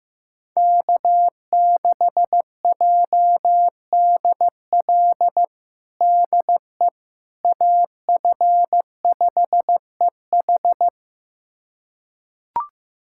Morse code, heard as a keyed tone, about 15 wpm, 700 Hz. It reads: K6JDL DE AF5EH
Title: USA Callsign DE Callsign - 5 Characters - 15wpm